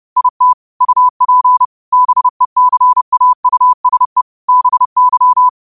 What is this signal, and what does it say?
Morse code, heard as a keyed tone, about 30 wpm, 1000 Hz. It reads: IT UP BECAUSE BY
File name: qotdfiles / 30wpm